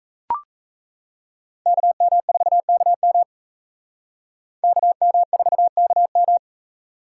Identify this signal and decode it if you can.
Morse code, heard as a keyed tone, about 35 wpm, 700 Hz. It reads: KM4KM KM4KM